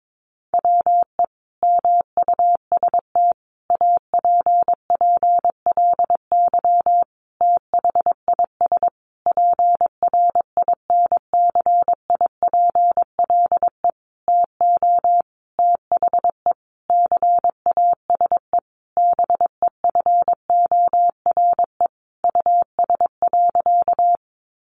Morse code, heard as a keyed tone, about 22 wpm, 700 Hz. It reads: WE MUST APPLY THIS PRINCIPLE TO THE CASE BEFORE US.
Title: Sentences from Top 1400 Words 22wpm